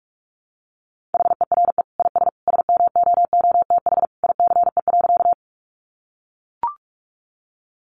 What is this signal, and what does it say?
Morse code, heard as a keyed tone, about 44 wpm, 700 Hz. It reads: HERE IS SMOOTH ICE.
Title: Sentences from Top 1400 Words 45wpm